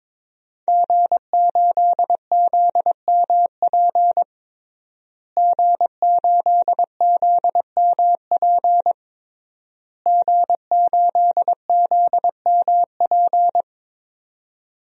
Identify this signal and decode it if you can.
Morse code, heard as a keyed tone, about 22 wpm, 700 Hz. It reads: G8ZMP G8ZMP G8ZMP
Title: DXCC-100 Call Signs - 5 Characters - Repeated 3x - 22wpm